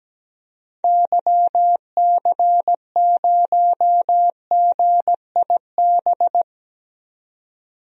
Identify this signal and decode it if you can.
Morse code, heard as a keyed tone, about 17 wpm, 700 Hz. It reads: YC0GIB
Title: DXCC-100 Call Signs - 6 Characters - 17wpm